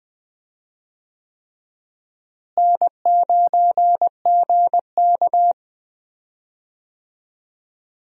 Morse code, heard as a keyed tone, about 21 wpm, 700 Hz. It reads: N9GK
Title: USA Call Signs - 4 Characters - Repeated 3x - 20wpm